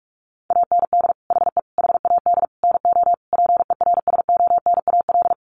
Morse code, heard as a keyed tone, about 44 wpm, 700 Hz. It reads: AND HE HAD NO PERSONAL
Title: Sentences from Top 1400 Words 45wpm